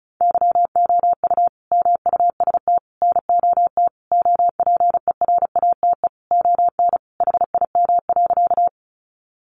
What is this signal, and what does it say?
Morse code, heard as a keyed tone, about 34 wpm, 700 Hz. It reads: YOU MUST NOT OPERATE ON HIM.
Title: Sentences from Top 1500 Words 35wpm